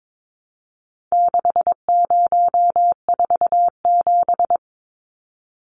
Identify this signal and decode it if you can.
Morse code, heard as a keyed tone, about 22 wpm, 700 Hz. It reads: 6047
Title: Numbers - 4-Digit 22wpm